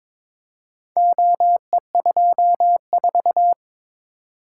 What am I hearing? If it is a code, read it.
Morse code, heard as a keyed tone, about 22 wpm, 700 Hz. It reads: OE24
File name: practice / 22wpm / Four Letter-Number Combinations - 22wpm